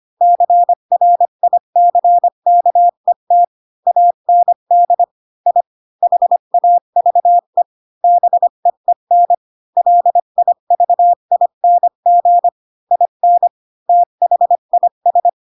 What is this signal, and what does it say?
Morse code, heard as a keyed tone, about 25 wpm, 700 Hz. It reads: CRICKET AND I HAVE BEEN LIVING IN THIS
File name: Pinocchio / Pinocchio ch4 20 25 ik1hge